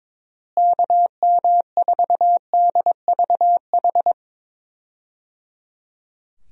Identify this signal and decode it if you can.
Morse code, heard as a keyed tone, about 22 wpm, 700 Hz. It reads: KM4DVH
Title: USA Call Signs - 6 Characters - Repeated 3x - 22wpm